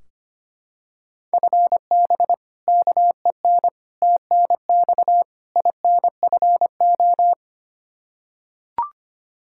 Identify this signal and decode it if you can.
Morse code, heard as a keyed tone, about 25 wpm, 700 Hz. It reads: FB KEN TNX INFO
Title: Practice QSO - Exchange 3 - More Info - 25wpm